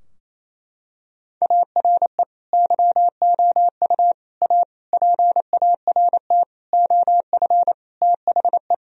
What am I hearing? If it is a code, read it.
Morse code, heard as a keyed tone, about 28 wpm, 700 Hz. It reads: ARE YOU A PART OF THE
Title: Sentences from Top 1000 Words - Review Part 4 - 28wpm